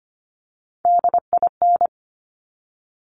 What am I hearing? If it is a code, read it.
Morse code, heard as a keyed tone, about 25 wpm, 700 Hz. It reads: DIN
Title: Top Three Letter Combinations in Words 25wpm